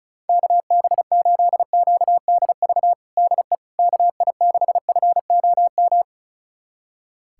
Morse code, heard as a keyed tone, about 36 wpm, 700 Hz. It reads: KB8QDV DE KI6FOM
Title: USA Callsign DE Callsign - 6 Characters - 35wpm